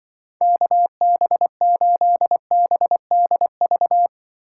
Morse code, heard as a keyed tone, about 23 wpm, 700 Hz. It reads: KB8BDV
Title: Speed Racing - USA 2x3 Call Signs 20wpm